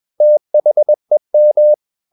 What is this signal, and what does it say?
Morse code, heard as a keyed tone, about 21 wpm, 600 Hz. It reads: THEM
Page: CW med Gnister